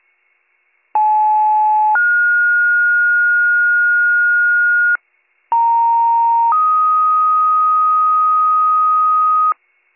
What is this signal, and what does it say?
Two-tone page, heard as a one-second tone, then a three-second tone, about 860 then 1430 Hz, about 930 then 1250 Hz, with noise. It is heard from 0.9 to 9.5 s.